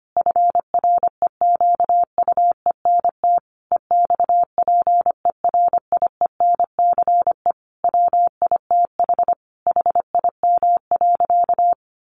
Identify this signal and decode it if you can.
Morse code, heard as a keyed tone, about 25 wpm, 700 Hz. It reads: FREQUENT EXPERIENCE WITH HIM.
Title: Sentences from Top 1400 Words 25wpm